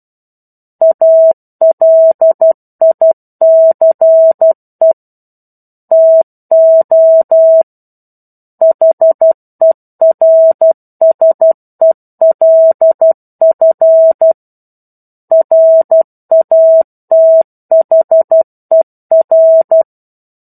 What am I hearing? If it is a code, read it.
Morse code, heard as a keyed tone, about 12 wpm, 650 Hz. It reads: ALICE TO HERSELF RATHER
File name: 01-Chapter-01 12 WPM